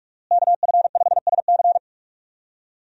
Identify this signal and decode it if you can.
Morse code, heard as a keyed tone, about 44 wpm, 700 Hz. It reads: KF5SC
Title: USA Call Signs - 5 Characters - Repeated 2x - 45wpm